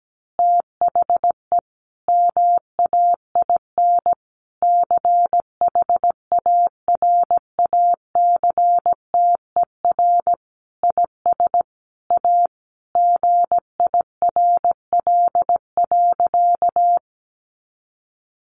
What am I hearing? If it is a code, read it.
Morse code, heard as a keyed tone, about 17 wpm, 700 Hz. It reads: THE MAIN CHARACTER IS A GIRL.
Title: Additional Easy Phrases 17wpm